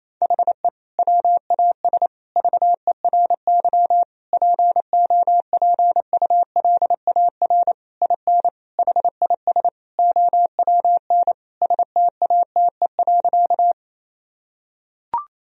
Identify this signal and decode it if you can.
Morse code, heard as a keyed tone, about 28 wpm, 700 Hz. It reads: HE WAS VERY POPULAR IN HIS OWN STATE.
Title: Sentences from Top 1000 Words - Review Part 3 - 28wpm